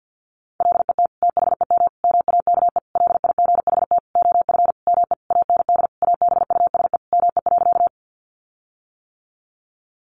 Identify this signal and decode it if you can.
Morse code, heard as a keyed tone, about 50 wpm, 700 Hz. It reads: LET THEM MAKE LIGHT OF ME AND ABUSE ME.
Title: Sentences from Top 1300 Words 50wpm